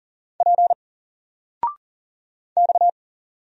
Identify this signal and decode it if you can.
Morse code, heard as a keyed tone, about 40 wpm, 700 Hz. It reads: P X
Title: Single Letter-Number 40wpm